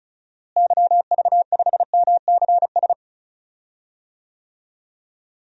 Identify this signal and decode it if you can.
Morse code, heard as a keyed tone, about 34 wpm, 700 Hz. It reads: YV5MCS
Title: DXCC-100 Call Signs - 6 Characters - Repeated 2x - 35wpm